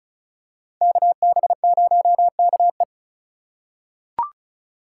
Morse code, heard as a keyed tone, about 34 wpm, 700 Hz. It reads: KB0KE
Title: USA Call Signs - 5 Characters - 4x Character Spacing - 35wpm